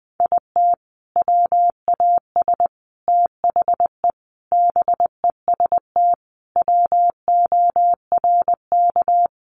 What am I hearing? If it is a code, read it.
Morse code, heard as a keyed tone, about 20 wpm, 700 Hz. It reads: IT WAS THE BEST WORK
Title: Sentences from Top 1300 Words 20wpm